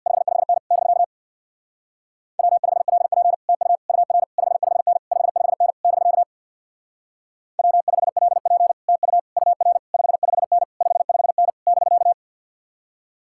Morse code, heard as a keyed tone, about 67 wpm, 700 Hz. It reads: 55N <BK> W5LP TU UR 55N 55N <BK> W5LP TU UR 55N 55N <BK>